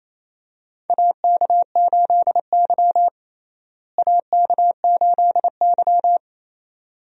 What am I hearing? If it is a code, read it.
Morse code, heard as a keyed tone, about 27 wpm, 700 Hz. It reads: AK8Y AK8Y
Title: USA Call Signs - 4 Characters - Repeated 2x - 28wpm